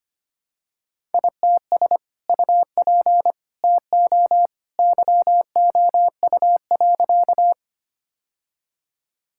Morse code, heard as a keyed tone, about 25 wpm, 700 Hz. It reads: ITS UP TO YOU.
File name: practice / 25wpm / Sentences with 2-Word Phrases - 25wpm